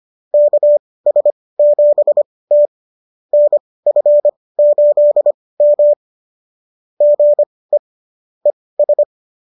Morse code, heard as a keyed tone, about 25 wpm, spaced as for 18 wpm, 590 Hz. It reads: KS7T NF8M GE ES